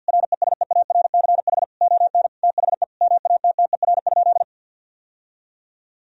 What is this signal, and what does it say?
Morse code, heard as a keyed tone, about 50 wpm, 700 Hz. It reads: RESEARCH ON THE MATTER?
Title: Sentences from Top 1000 Words - Review Part 4 - 50wpm